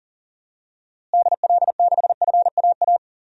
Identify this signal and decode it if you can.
Morse code, heard as a keyed tone, about 40 wpm, 700 Hz. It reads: DL6FAA
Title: DXCC-100 Call Signs - 6 Characters - Repeated 3x - 40wpm